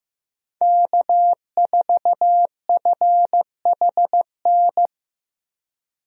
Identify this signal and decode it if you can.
Morse code, heard as a keyed tone, about 15 wpm, 700 Hz. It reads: K4FHN
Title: USA Call Signs - 5 Characters - Repeated 2x - 15wpm